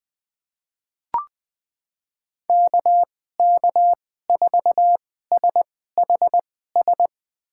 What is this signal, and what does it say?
Morse code, heard as a keyed tone, about 20 wpm, spaced as for 15 wpm, 700 Hz. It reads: KK4SHS
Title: USA Call Signs - 6 Characters - 2x Character Spacing - 20wpm